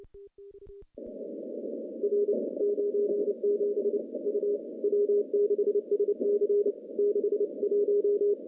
Morse code, heard as a keyed tone, about 27 wpm, 410 Hz. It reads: R 97 U W6SC 61
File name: CW-SS-k5tr